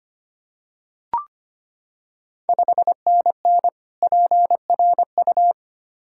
Morse code, heard as a keyed tone, about 25 wpm, 700 Hz. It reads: 5NN PRU
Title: Ontario QSO Party Exchange 25wpm